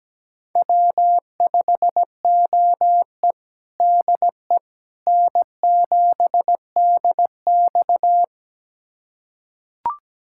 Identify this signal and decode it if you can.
Morse code, heard as a keyed tone, about 17 wpm, 700 Hz. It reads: W5OE DE N7DX
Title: USA Callsign DE Callsign - 4 Characters - 17wpm